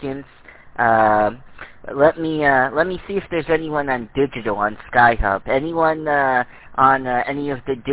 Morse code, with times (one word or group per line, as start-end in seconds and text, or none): none